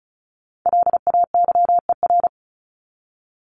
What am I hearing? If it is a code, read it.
Morse code, heard as a keyed tone, about 34 wpm, 700 Hz. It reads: LAYER
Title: Top 1501-1750 Words 35wpm